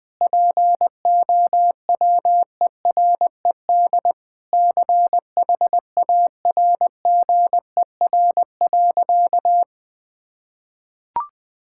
Morse code, heard as a keyed tone, about 20 wpm, 700 Hz. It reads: POWERED CHARGER.
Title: Sentences with 6-Word Phrases - 20wpm